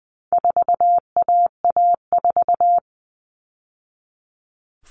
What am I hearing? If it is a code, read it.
Morse code, heard as a keyed tone, about 20 wpm, 700 Hz. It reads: 4AA4
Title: All Cut Numbers - 4-Digit 20wpm